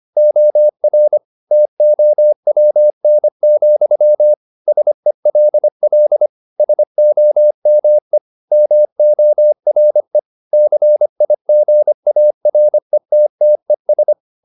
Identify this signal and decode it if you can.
Morse code, heard as a keyed tone, about 25 wpm, 600 Hz. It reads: OR TOWN, SELL SOME MORE CIGARETTES